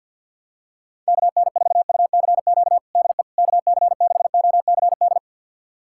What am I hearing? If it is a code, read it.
Morse code, heard as a keyed tone, about 46 wpm, 700 Hz. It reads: KN4UCQ DE KC6OCD
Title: USA Callsign DE Callsign - 6 Characters - 50wpm